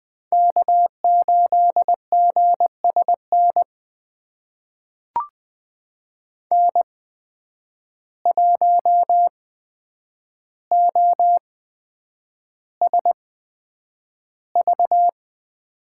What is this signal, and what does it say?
Morse code, heard as a keyed tone, about 20 wpm, spaced as for 9 wpm, 700 Hz. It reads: K8GSN N1OSV
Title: USA Call Signs - 5 Characters - 8x Character Spacing - 20wpm